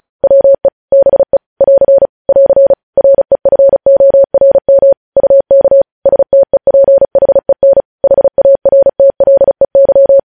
Morse code, heard as a keyed tone, about 36 wpm, 550 Hz. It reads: WE BE <AR> <AR> REFORM UK STEPHEN HARTLEY
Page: news headlines in morse code at 35 wpm